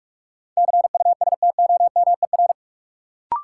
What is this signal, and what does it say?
Morse code, heard as a keyed tone, about 45 wpm, 700 Hz. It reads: CUSTOMER
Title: Sentences from Top 1100 Words - New Word Drill - 45wpm